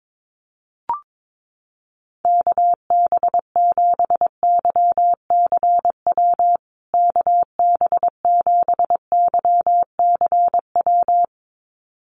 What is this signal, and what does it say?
Morse code, heard as a keyed tone, about 22 wpm, 700 Hz. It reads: KB7YCW KB7YCW
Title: USA Call Signs Repeated 2x 22wpm